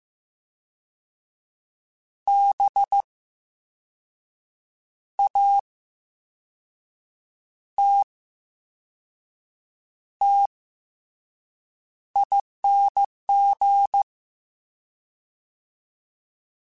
Morse code, heard as a keyed tone, about 15 wpm, spaced as for 5 wpm, 800 Hz. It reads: BATTING
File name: Special INGs / Special INGs-mp3s / Slow ing @13